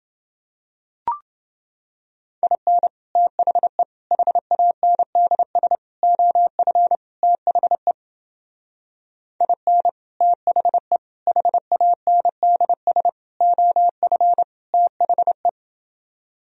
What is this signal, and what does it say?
Morse code, heard as a keyed tone, about 30 wpm, 700 Hz. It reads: IN THE HANDS OF THE IN THE HANDS OF THE